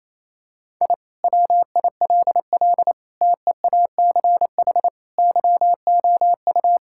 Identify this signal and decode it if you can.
Morse code, heard as a keyed tone, about 28 wpm, 700 Hz. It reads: I WILL TEACH YOU
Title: Sentences from Top 1000 Words - Review Part 6 - 28wpm